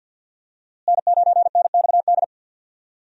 Morse code, heard as a keyed tone, about 47 wpm, 700 Hz. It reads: N9NXD
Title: USA Call Signs - 5 Characters - Repeated 2x - 50wpm